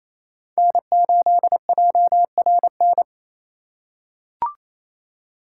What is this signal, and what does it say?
Morse code, heard as a keyed tone, about 28 wpm, 700 Hz. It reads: N8JRN
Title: USA Call Signs - 5 Characters - 4x Character Spacing - 28wpm